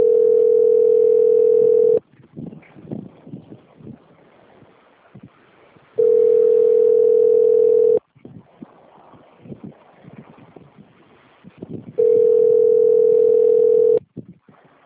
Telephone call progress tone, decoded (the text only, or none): ringback tone